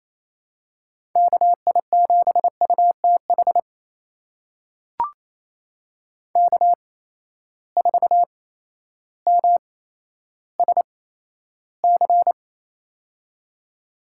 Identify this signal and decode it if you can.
Morse code, heard as a keyed tone, about 28 wpm, spaced as for 13 wpm, 700 Hz. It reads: KI7UTH K4MSC
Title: USA Call Signs - 8x Character Spacing - 28wpm